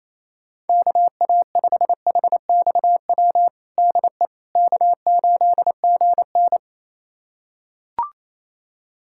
Morse code, heard as a keyed tone, about 28 wpm, 700 Hz. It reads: KA5HXW DE K8GN
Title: USA Callsign DE Callsign 28wpm